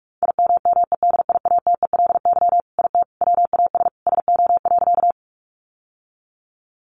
Morse code, heard as a keyed tone, about 44 wpm, 700 Hz. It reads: IMMEDIATELY IT WAS SO.
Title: Sentences from Top 1100 Words 45wpm